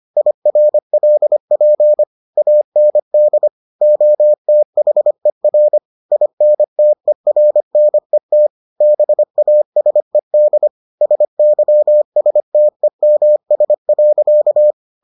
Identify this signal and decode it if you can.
Morse code, heard as a keyed tone, about 25 wpm, 600 Hz. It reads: IRLP AND OTHER INTERNET BASED SYSTEMS.